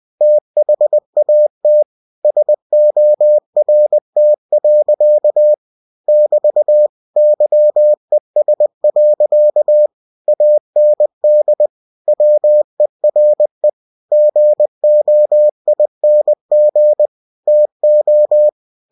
Morse code, heard as a keyed tone, about 20 wpm, 600 Hz. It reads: THAT SORT. <BT> YES. AND WERE GOING TO